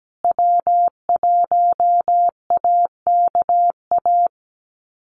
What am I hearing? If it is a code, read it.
Morse code, heard as a keyed tone, about 17 wpm, 700 Hz. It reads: W1AKA
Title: USA Call Signs Repeated 3x 17wpm